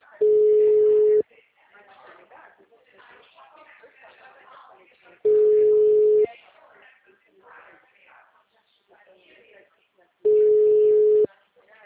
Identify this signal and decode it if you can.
Ringback tone (Europe, 425 Hz), with speech.